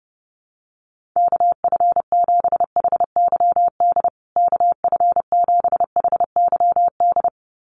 Morse code, heard as a keyed tone, about 30 wpm, 700 Hz. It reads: KF7HYD KF7HYD
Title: USA Call Signs Repeated 2x 30wpm